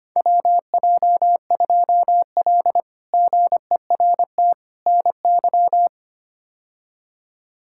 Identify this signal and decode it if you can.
Morse code, heard as a keyed tone, about 25 wpm, 700 Hz. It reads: WJ2L GERT NY
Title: K1USN SST QSO Practice - 25wpm